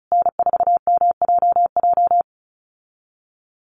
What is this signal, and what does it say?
Morse code, heard as a keyed tone, about 34 wpm, 700 Hz. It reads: N4MJJ
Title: USA Call Signs Repeated 3x 35wpm